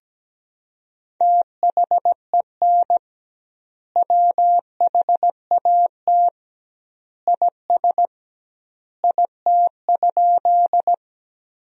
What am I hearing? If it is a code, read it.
Morse code, heard as a keyed tone, about 17 wpm, 700 Hz. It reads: THEN WHAT IS IT?
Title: 2x Word-Spacing - Sentences from Top 100 Words - 17wpm